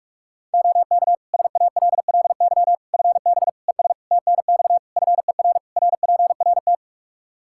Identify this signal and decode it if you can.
Morse code, heard as a keyed tone, about 46 wpm, 700 Hz. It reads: OK SALLY FB ES TNX FER RPRT
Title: Practice QSO - Exchange 2 - Info - 45wpm